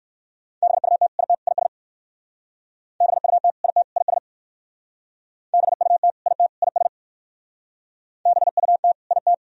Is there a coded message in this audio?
Yes: Morse code from 0.6 to 9.3 s, heard as a keyed tone, about 57 wpm, 700 Hz. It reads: BUT IT IS BUT IT IS BUT IT IS BUT IT